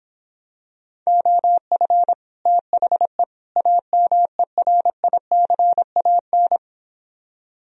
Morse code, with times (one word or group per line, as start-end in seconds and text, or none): 1.1-2.1 OF
2.5-3.2 THE
3.6-6.6 AMERICAN